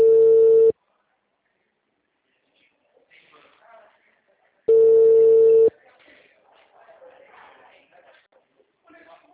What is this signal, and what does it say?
Ringback tone (China, 450 Hz), with speech.